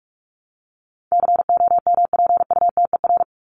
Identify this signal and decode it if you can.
Morse code, heard as a keyed tone, about 44 wpm, 700 Hz. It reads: COMPUTER
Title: Top 501-750 Words 45wpm